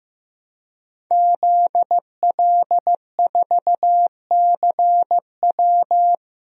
Morse code, heard as a keyed tone, about 15 wpm, 700 Hz. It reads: ZL4CW
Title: DXCC-100 Call Signs - 5 Characters - Repeated 2x - 15wpm